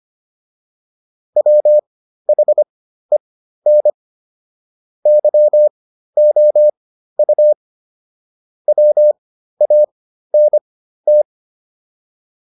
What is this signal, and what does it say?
Morse code, heard as a keyed tone, about 25 wpm, spaced as for 13 wpm, 600 Hz. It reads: WHEN YOU WANT